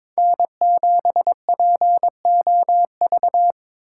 Morse code, heard as a keyed tone, about 27 wpm, 700 Hz. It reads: N7POV
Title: Speed Racing - USA Call Signs 22wpm